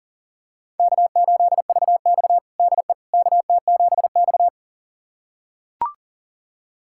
Morse code, heard as a keyed tone, about 40 wpm, 700 Hz. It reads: K8VX DE KT7X
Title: USA Callsign DE Callsign - 4 Characters - 40wpm